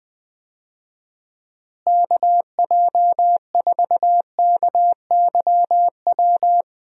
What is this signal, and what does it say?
Morse code, heard as a keyed tone, about 20 wpm, 700 Hz. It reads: KJ4KYW
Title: USA Call Signs - 6 Characters - Repeated 3x - 20wpm